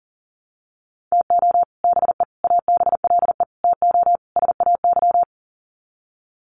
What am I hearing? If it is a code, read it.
Morse code, heard as a keyed tone, about 37 wpm, 700 Hz. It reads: TO BE ABLE TO SAY
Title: Sets of 5 Words Encore2 40wpm